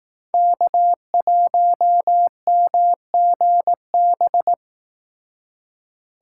Morse code, heard as a keyed tone, about 18 wpm, 700 Hz. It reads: K1MGB
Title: Speed Racing - USA Call Signs 15wpm